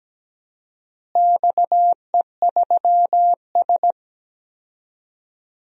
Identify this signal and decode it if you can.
Morse code, heard as a keyed tone, about 17 wpm, 700 Hz. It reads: XE3S
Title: DXCC-100 Call Signs - 4 Characters - Repeated 3x - 17wpm